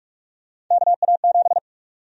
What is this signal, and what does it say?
Morse code, heard as a keyed tone, about 46 wpm, 700 Hz. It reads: KA7
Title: USA Call Sign Prefixes - 45wpm